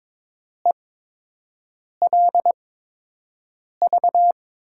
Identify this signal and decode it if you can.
Morse code, heard as a keyed tone, about 22 wpm, spaced as for 6 wpm, 700 Hz. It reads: ELV
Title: USA Call Signs - 6 Characters - 8x Character Spacing - 22wpm